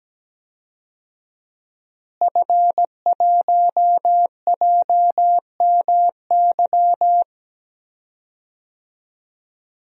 Morse code, heard as a keyed tone, about 17 wpm, 700 Hz. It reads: F1JMY
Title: DXCC-100 Call Signs - 5 Characters - Repeated 2x - 17wpm